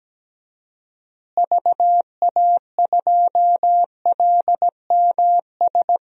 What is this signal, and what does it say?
Morse code, heard as a keyed tone, about 17 wpm, 700 Hz. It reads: VA2LMS
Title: DXCC-100 Call Signs - 6 Characters - Repeated 2x - 17wpm